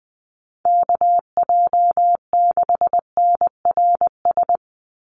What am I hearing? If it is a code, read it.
Morse code, heard as a keyed tone, about 19 wpm, 700 Hz. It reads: KJ6NRS
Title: USA Call Signs 20wpm